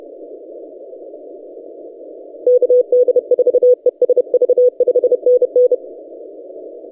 Morse code, heard as a keyed tone, about 30 wpm, 500 Hz. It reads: KD4ESV 5NN
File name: posts / 4w-n1yc-17cw- -1516z